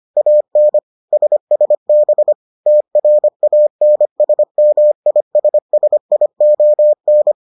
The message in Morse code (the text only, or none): AN SSB TRANSMISSION